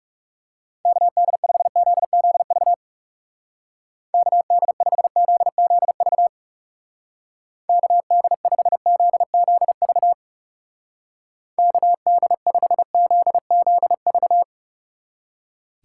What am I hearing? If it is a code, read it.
Morse code, heard as a keyed tone, about 38 wpm, 700 Hz. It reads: KD5ZZV KD5ZZV KD5ZZV KD5ZZV